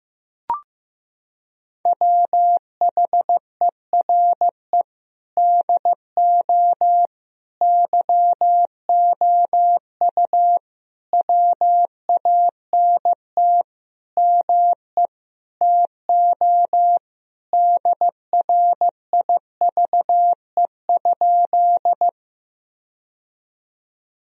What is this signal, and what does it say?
Morse code, heard as a keyed tone, about 15 wpm, 700 Hz. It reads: WHERE DO YOU WANT ME TO DRIVE?
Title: Sentences from Top 1000 Words - Review Part 2 - 15wpm